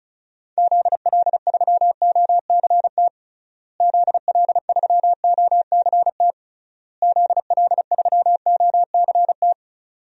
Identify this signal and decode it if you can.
Morse code, heard as a keyed tone, about 34 wpm, 700 Hz. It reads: ZL3OCT ZL3OCT ZL3OCT
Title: DXCC-100 Call Signs - 6 Characters - Repeated 3x - 35wpm